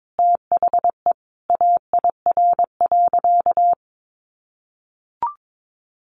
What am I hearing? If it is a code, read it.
Morse code, heard as a keyed tone, about 22 wpm, 700 Hz. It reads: THE AIR.
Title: Easy Phrases 22wpm